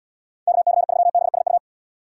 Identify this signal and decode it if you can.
Morse code, heard as a keyed tone, about 77 wpm, 700 Hz. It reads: KC3BSU